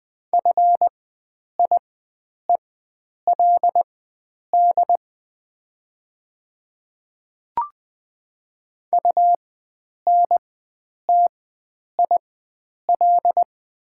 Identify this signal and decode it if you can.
Morse code, heard as a keyed tone, about 20 wpm, spaced as for 9 wpm, 700 Hz. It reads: FIELD UNTIL
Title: 4x Character Spacing - Top 500 Words - 20wpm